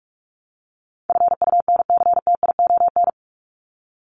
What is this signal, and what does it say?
Morse code, heard as a keyed tone, about 44 wpm, 700 Hz. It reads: FUNCTION
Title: ES:S Suffix - Top 100 Words with Base Words 45wpm